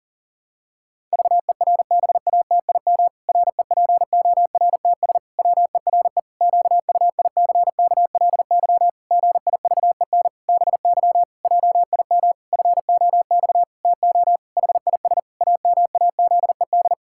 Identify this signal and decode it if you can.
Morse code, heard as a keyed tone, about 40 wpm, 700 Hz. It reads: VERBATIM REPORTS WERE QUICKLY GIVEN BY JIM FOX TO HIS AMAZED
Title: Pangram Challenge v2 - 40wpm